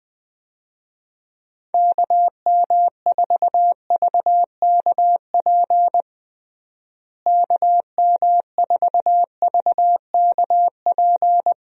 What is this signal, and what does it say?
Morse code, heard as a keyed tone, about 20 wpm, 700 Hz. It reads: KM4VKP KM4VKP
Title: USA Call Signs - 6 Characters - Repeated 2x - 20wpm